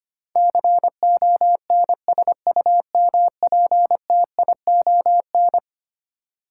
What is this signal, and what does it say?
Morse code, heard as a keyed tone, about 25 wpm, 700 Hz. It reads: CONSUMPTION
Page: morse-code-ninja-practice-25wpm